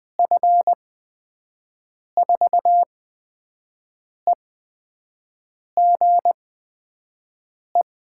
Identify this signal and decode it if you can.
Morse code, heard as a keyed tone, about 20 wpm, spaced as for 5 wpm, 700 Hz. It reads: F4EGE
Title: USA Call Signs - 6 Characters - 8x Character Spacing - 20wpm